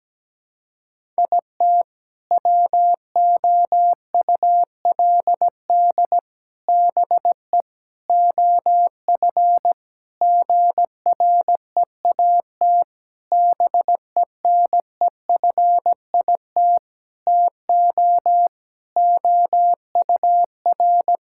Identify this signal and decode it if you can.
Morse code, heard as a keyed tone, about 17 wpm, 700 Hz. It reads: IT WOULD BE OF GREAT BENEFIT TO OUR
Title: Sentences from Top 1000 Words - Review Part 6 - 17wpm